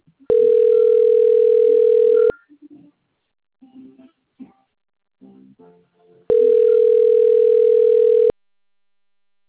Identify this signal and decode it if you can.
Ringback tone (North America), 440 + 480 Hz.
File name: rg-1600-unknown- - - .28812